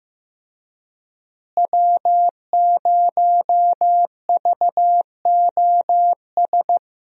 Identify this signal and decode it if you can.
Morse code, heard as a keyed tone, about 15 wpm, 700 Hz. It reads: W0VOS
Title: USA Call Signs - 5 Characters - Repeated 3x - 15wpm